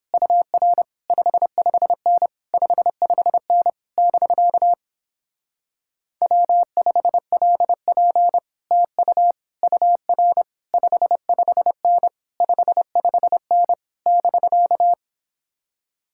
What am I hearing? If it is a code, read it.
Morse code, heard as a keyed tone, about 29 wpm, 700 Hz. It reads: UR 55N 55N <BK> W5LP TU UR 55N 55N <BK>